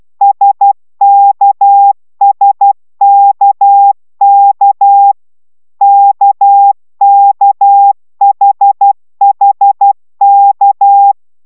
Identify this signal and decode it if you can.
Morse code, heard as a keyed tone, about 12 wpm, 800 Hz. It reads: SKSKK KKHHK